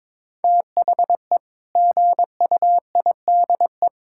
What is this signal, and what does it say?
Morse code, heard as a keyed tone, about 22 wpm, 700 Hz. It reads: THE GUIDE
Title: Sentences from Top 1000 Words - Review Part 6 - 22wpm